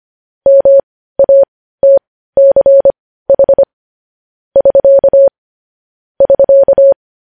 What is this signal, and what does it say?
Morse code, heard as a keyed tone, about 25 wpm, spaced as for 15 wpm, 550 Hz. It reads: MATCH <SK> <SK>
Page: news headlines in morse code at 15 wpm